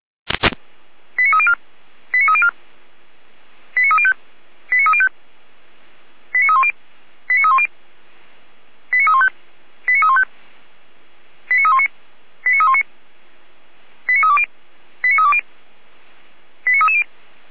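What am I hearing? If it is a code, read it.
Five-tone selective call (ZVEI), heard as quick runs of five tones, tones of about 70 ms. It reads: 89304 89304 89305 89305 89310 89310 89315 89315 89319 89319 89320 89320 89330
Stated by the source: ZVEI five-tone selective call received on SDR, from Feuerwehr Orlamünde.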